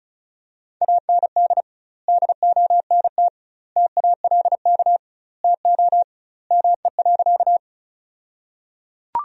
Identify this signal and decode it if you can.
Morse code, heard as a keyed tone, about 35 wpm, 700 Hz. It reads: AND DONT TALK TO ME.
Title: 2x Word-Spacing - Sentences from Top 400 Words - 35wpm